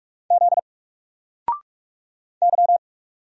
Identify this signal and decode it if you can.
Morse code, heard as a keyed tone, about 44 wpm, 700 Hz. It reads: Z Y
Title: Single Letter-Number 45wpm